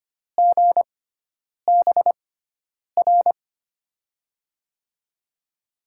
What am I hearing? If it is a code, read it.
Morse code, heard as a keyed tone, about 25 wpm, spaced as for 9 wpm, 700 Hz. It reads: GBR
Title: USA Call Signs - 6x Character Spacing - 25wpm